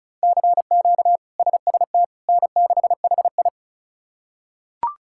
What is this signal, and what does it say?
Morse code, heard as a keyed tone, about 34 wpm, 700 Hz. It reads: CQ SST N6HI
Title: K1USN SST QSO Practice - 35wpm